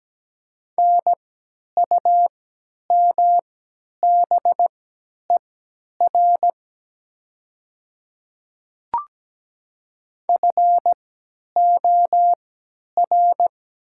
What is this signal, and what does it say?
Morse code, heard as a keyed tone, about 17 wpm, spaced as for 10 wpm, 700 Hz. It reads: NUMBER FOR
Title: 3x Character Spacing - Top 300 Words - 17wpm